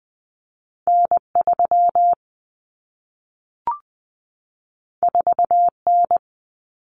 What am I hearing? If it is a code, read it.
Morse code, heard as a keyed tone, about 20 wpm, 700 Hz. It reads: N3 4N
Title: Common Cut Numbers - 2-Digit 20wpm